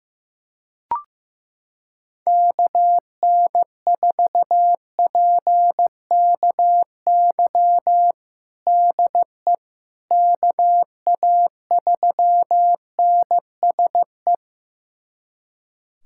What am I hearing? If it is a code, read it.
Morse code, heard as a keyed tone, about 16 wpm, 700 Hz. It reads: KN4PKY DE KA3NSE
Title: USA Callsign DE Callsign - 6 Characters - 15wpm